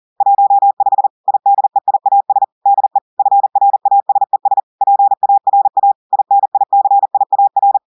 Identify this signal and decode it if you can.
Morse code, heard as a keyed tone, about 40 wpm, 800 Hz. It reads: 15 IDEIAS DE FRASES PARA INICIAR
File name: audios / portugais / portugais texte en clair 2 vitesse 40 mots minutes